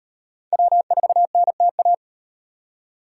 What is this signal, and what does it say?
Morse code, heard as a keyed tone, about 38 wpm, 700 Hz. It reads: W4NTA